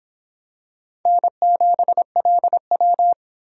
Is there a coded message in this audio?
Yes: Morse code from 1.1 to 3.1 s, heard as a keyed tone, about 25 wpm, 700 Hz. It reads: N7LW